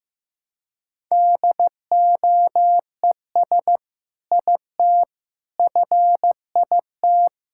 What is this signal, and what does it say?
Morse code, heard as a keyed tone, about 16 wpm, 700 Hz. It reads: DOES IT FIT
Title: Sentences from Top 1000 Words - Review Part 5 - 15wpm